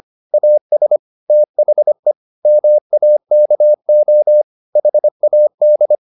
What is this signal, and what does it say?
Morse code, heard as a keyed tone, about 25 wpm, 600 Hz. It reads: AS THE MAKO HAD